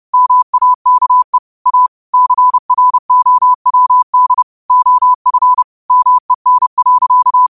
Morse code, heard as a keyed tone, about 30 wpm, 1000 Hz. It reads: MAKE A CROWD OF MEN.
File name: qotdfiles / 30wpm